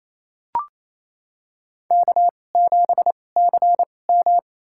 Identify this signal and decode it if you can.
Morse code, heard as a keyed tone, about 28 wpm, spaced as for 20 wpm, 700 Hz. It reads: K7CM
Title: USA Call Signs - 4 Characters - 2x Character Spacing - 28wpm